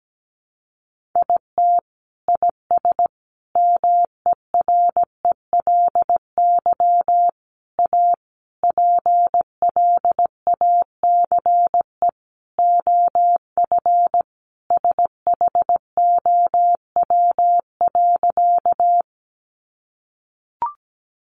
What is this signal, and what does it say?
Morse code, heard as a keyed tone, about 17 wpm, 700 Hz. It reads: IT IS MERELY A PLACE OF SHOW.
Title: Sentences from Top 1400 Words 17wpm